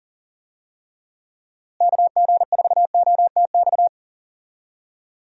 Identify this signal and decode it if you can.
Morse code, heard as a keyed tone, about 40 wpm, 700 Hz. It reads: KG4OTX
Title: USA Call Signs - 6 Characters - Repeated 3x - 40wpm